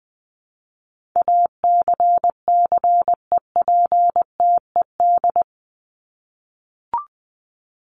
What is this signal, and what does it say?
Morse code, heard as a keyed tone, about 20 wpm, 700 Hz. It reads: ACCEPTED
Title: ED Suffix - Top 100 Words with Base Words 20wpm